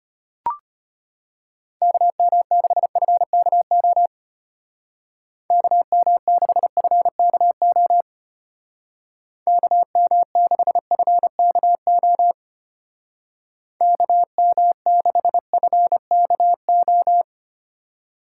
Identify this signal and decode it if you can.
Morse code, heard as a keyed tone, about 32 wpm, 700 Hz. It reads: KM6FKO KM6FKO KM6FKO KM6FKO